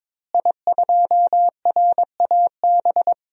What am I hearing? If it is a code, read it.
Morse code, heard as a keyed tone, about 22 wpm, 700 Hz. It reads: I2RAB
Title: DXCC-100 Call Signs - 5 Characters - Repeated 3x - 22wpm